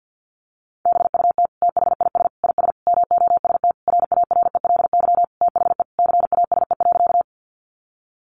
Morse code, heard as a keyed tone, about 50 wpm, 700 Hz. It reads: BUT THIS IS MOST RARELY THE CASE.
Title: Sentences from Top 1400 Words 50wpm